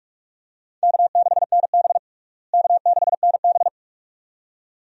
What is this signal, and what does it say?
Morse code, heard as a keyed tone, about 46 wpm, 700 Hz. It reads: K6NB K6NB
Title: USA Call Signs - 4 Characters - Repeated 2x - 45wpm